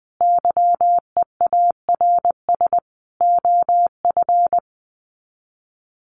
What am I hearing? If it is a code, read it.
Morse code, heard as a keyed tone, about 20 wpm, 700 Hz. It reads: YEARS OF
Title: Sets of 5 Words Encore 20wpm